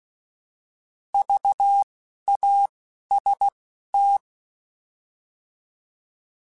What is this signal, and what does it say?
Morse code, heard as a keyed tone, about 16 wpm, spaced as for 11 wpm, 800 Hz. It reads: VAST
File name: Alphabetical / Alphabetical mp3s / Vs @10